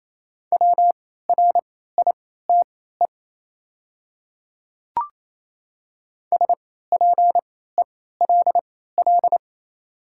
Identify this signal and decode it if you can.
Morse code, heard as a keyed tone, about 28 wpm, spaced as for 16 wpm, 700 Hz. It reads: WRITE SPELL
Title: 3x Character Spacing - Top 200 Words - 28wpm